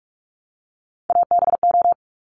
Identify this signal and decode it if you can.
Morse code, heard as a keyed tone, about 46 wpm, 690 Hz. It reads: ABO
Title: Top Three Letter Combinations in Words 45wpm